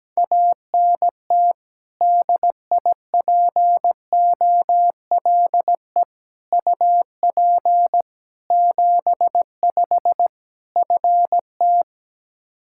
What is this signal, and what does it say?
Morse code, heard as a keyed tone, about 17 wpm, 700 Hz. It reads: ANT DIPOLE UP 75 FT
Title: Practice QSO - Exchange 2 - Info - 17wpm